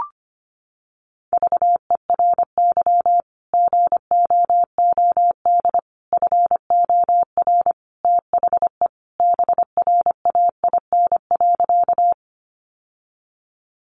Morse code, heard as a keyed tone, about 25 wpm, 700 Hz. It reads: VERY GOOD FOR THE BRAIN.
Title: Sentences from Top 1100 Words 25wpm